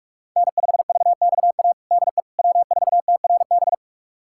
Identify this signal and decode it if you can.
Morse code, heard as a keyed tone, about 42 wpm, 700 Hz. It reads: N5VXA DE W4TRB
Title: USA Callsign DE Callsign - 5 Characters - 45wpm